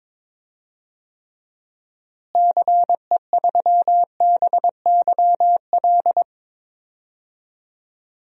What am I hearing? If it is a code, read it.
Morse code, heard as a keyed tone, about 22 wpm, 700 Hz. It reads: CE3BYL
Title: DXCC-100 Call Signs - 6 Characters - Repeated 3x - 22wpm